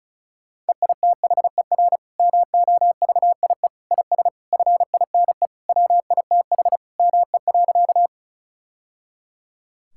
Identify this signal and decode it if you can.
Morse code, heard as a keyed tone, about 36 wpm, 700 Hz. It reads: EITHER MOVIE IS FINE WITH ME.
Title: Sentences from Top 1000 Words - Review Part 5 - 35wpm